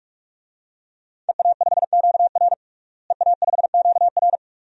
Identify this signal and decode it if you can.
Morse code, heard as a keyed tone, about 46 wpm, 700 Hz. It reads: EA5QR EA5QR
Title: DXCC-100 Call Signs - 5 Characters - Repeated 2x - 45wpm